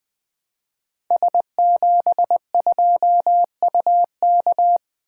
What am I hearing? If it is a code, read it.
Morse code, heard as a keyed tone, about 20 wpm, 700 Hz. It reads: S72UK
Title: Code Groups - 5 Letter:Numbers - 20wpm